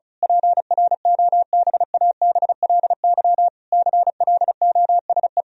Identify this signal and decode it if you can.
Morse code, heard as a keyed tone, about 36 wpm, 700 Hz. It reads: PROBABLY CLOSE
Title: Sentences from Top 1000 Words - Review Part 4 - 35wpm